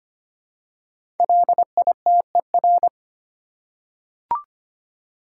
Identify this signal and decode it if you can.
Morse code, heard as a keyed tone, about 25 wpm, 700 Hz. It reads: LITER
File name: 5 Letter Words - First 3 Letters Given - 25wpm